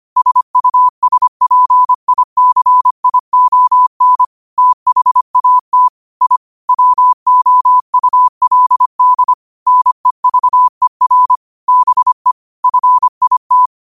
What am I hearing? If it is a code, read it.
Morse code, heard as a keyed tone, about 25 wpm, 1000 Hz. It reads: SUSPICION THAT I WOULD NEVER BE FIT
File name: qotdfiles / 25wpm